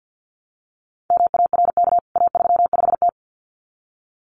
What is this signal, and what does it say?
Morse code, heard as a keyed tone, about 50 wpm, 700 Hz. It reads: MARK A35T
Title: CWT Exchange 50wpm